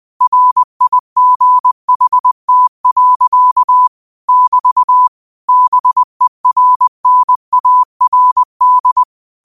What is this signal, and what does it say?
Morse code, heard as a keyed tone, about 20 wpm, 1000 Hz. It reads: RIGHT. <BT> BERNARD
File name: 20wpm